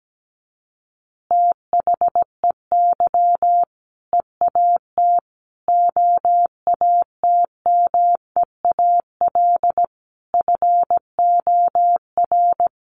Morse code, heard as a keyed tone, about 17 wpm, 700 Hz. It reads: THEY EAT OATMEAL FOR
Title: Easy Phrases 17wpm